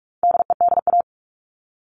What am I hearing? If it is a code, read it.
Morse code, heard as a keyed tone, about 44 wpm, 700 Hz. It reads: DEDA
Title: All Cut Numbers - 4-Digit 45wpm